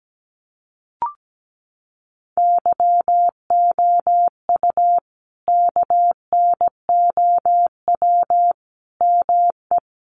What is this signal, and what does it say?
Morse code, heard as a keyed tone, about 17 wpm, 700 Hz. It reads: YOU KNOW ME
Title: Sentences from Top 400 Words 17wpm